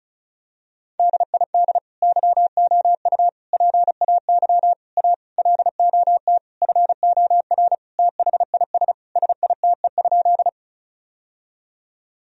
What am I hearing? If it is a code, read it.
Morse code, heard as a keyed tone, about 34 wpm, 700 Hz. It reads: DID YOU PAY A LOT FOR THIS SITE?
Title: Sentences from Top 1000 Words - Review Part 4 - 35wpm